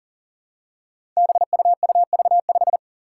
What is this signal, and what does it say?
Morse code, heard as a keyed tone, about 39 wpm, 700 Hz. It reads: BUUV5
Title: Five Challenging Letter-Number Combinations 40wpm